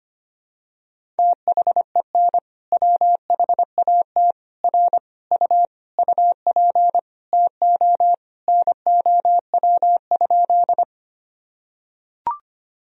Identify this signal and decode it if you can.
Morse code, heard as a keyed tone, about 25 wpm, 700 Hz. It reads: THEN WHAT R U UP TO NOW?
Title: Sentences from Top 100 Words and Abbreviations 25wpm